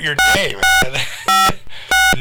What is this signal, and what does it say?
DTMF, heard as telephone touch tones, tones of about 180 ms, with speech. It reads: 6656